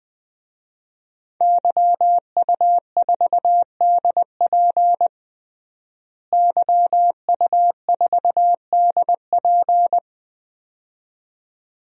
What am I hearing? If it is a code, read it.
Morse code, heard as a keyed tone, about 20 wpm, 700 Hz. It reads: YU4DP YU4DP
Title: DXCC-100 Call Signs - 5 Characters - Repeated 2x - 20wpm